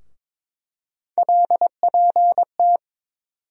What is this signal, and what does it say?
Morse code, heard as a keyed tone, about 21 wpm, 700 Hz. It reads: LPT
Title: USA Call Sign Postfixes - 3 Letters 22wpm